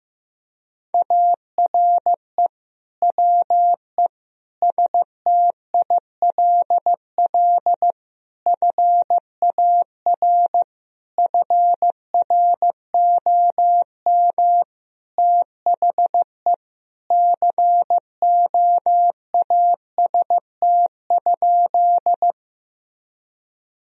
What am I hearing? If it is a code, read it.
Morse code, heard as a keyed tone, about 15 wpm, 700 Hz. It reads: ARE WE STILL FAR FROM THE COAST?
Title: Sentences from Top 1000 Words - Review Part 4 - 15wpm